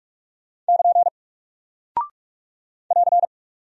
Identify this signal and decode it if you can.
Morse code, heard as a keyed tone, about 44 wpm, 700 Hz. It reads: <KN> <AR>
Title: Single Character (Letter:Number:Punctuation:Prosign) 45wpm